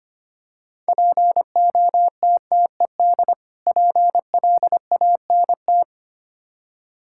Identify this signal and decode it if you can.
Morse code, heard as a keyed tone, about 25 wpm, 700 Hz. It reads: POTTED PLANT
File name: practice / 25wpm / Difficult Words with Hints - 25wpm